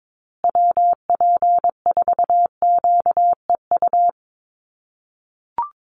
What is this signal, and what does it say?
Morse code, heard as a keyed tone, about 22 wpm, 700 Hz. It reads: WP4QEU
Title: USA Call Signs Repeated 3x 22wpm